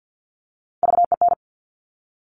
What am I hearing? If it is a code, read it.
Morse code, heard as a keyed tone, about 50 wpm, 700 Hz. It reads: VEN
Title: Top Three Letter Combinations in Words 50wpm